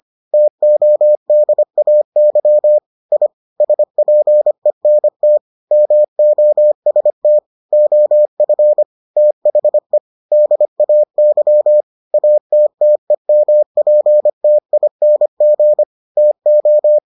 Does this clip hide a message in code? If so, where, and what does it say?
Yes: Morse code from 0.3 to 17.0 s, heard as a keyed tone, about 25 wpm, 600 Hz. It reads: TODAY I SPENT MOST OF THE DAY ATTEMPTING TO